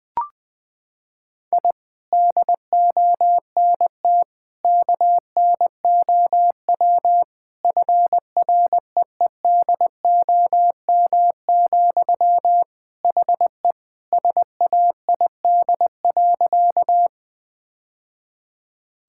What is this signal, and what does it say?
Morse code, heard as a keyed tone, about 20 wpm, 700 Hz. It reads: I DONT KNOW FREEDOM, HE SAID.
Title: Sentences from Top 1000 Words - Review Part 5 - 20wpm